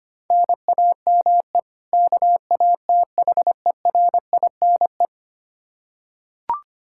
Morse code, heard as a keyed tone, about 25 wpm, 700 Hz. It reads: NAME KATHERINE
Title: Practice QSO - Exchange 1 - Beginning - 25wpm